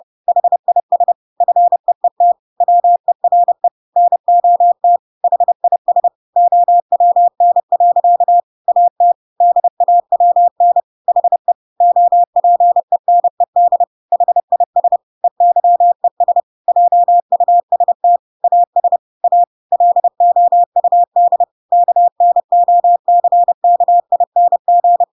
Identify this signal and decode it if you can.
Morse code, heard as a keyed tone, about 30 wpm, 700 Hz. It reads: HIS FEET WERE NOT HIS OWN. AT DAWN HE OPENED HIS EYES JUST AS A LOUD KNOCKING
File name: Pinocchio / Pinocchio ch6 30 30 ik1hge